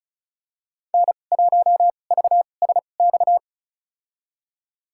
Morse code, heard as a keyed tone, about 36 wpm, spaced as for 25 wpm, 700 Hz. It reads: N1VSX
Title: USA Call Signs - 2x Character Spacing - 35wpm